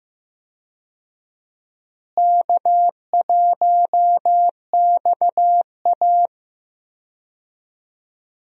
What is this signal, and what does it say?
Morse code, heard as a keyed tone, about 16 wpm, 700 Hz. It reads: K1XA
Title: USA Call Signs - 4 Characters - Repeated 2x - 15wpm